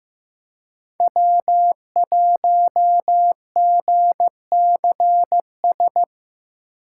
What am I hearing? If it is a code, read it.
Morse code, heard as a keyed tone, about 15 wpm, 700 Hz. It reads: W1GCS
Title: K1USN SST QSO Practice - 15wpm